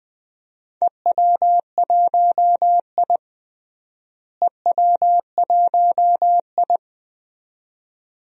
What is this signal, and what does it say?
Morse code, heard as a keyed tone, about 20 wpm, 700 Hz. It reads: EW1I EW1I
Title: DXCC-100 Call Signs - 4 Characters - Repeated 2x - 20wpm